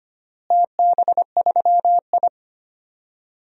Morse code, heard as a keyed tone, about 25 wpm, 700 Hz. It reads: TB3I
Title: DXCC-100 Call Signs - 4 Characters - Repeated 3x - 25wpm